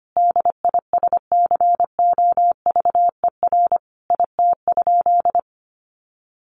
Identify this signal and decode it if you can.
Morse code, heard as a keyed tone, about 25 wpm, 700 Hz. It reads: DISCOVER IT?
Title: Sentences from Top 1100 Words 25wpm